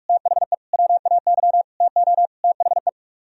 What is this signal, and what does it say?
Morse code, heard as a keyed tone, about 49 wpm, 700 Hz. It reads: THE WAY TO THE